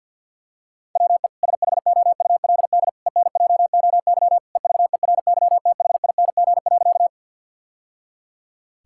Morse code, heard as a keyed tone, about 50 wpm, 700 Hz. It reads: WE SHOULD ENJOY EVERYTHING.
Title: Sentences from Top 1000 Words - Review Part 5 - 50wpm